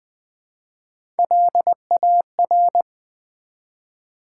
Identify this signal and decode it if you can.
Morse code, heard as a keyed tone, about 20 wpm, 700 Hz. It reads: LAR
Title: Build A Word - 1 Letter At A Time - 3 to 5-Letter Words - 20wpm